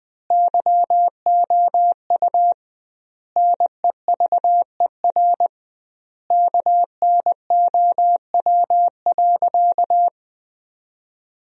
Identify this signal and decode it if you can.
Morse code, heard as a keyed tone, about 20 wpm, 700 Hz. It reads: YOU NEVER KNOW.
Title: 2x Word-Spacing - Sentences from Top 300 Words - 20wpm